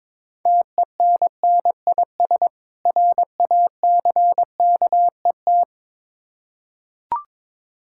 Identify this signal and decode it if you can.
Morse code, heard as a keyed tone, about 22 wpm, 700 Hz. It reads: TENNIS RACKET
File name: Difficult Words with Hints - 22wpm